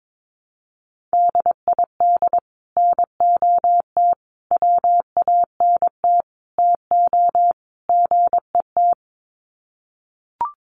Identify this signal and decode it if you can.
Morse code, heard as a keyed tone, about 22 wpm, 700 Hz. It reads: DID NOT WANT TO GET
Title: Sets of 5 Words Encore2 22wpm